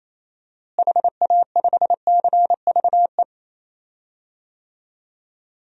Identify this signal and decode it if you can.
Morse code, heard as a keyed tone, about 28 wpm, 700 Hz. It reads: HA5CVE
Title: DXCC-100 Call Signs - 6 Characters - Repeated 2x - 28wpm